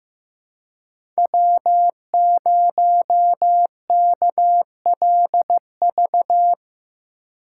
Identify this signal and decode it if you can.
Morse code, heard as a keyed tone, about 18 wpm, 700 Hz. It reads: W0KLV
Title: Speed Racing - USA 1x3 Call Signs 15wpm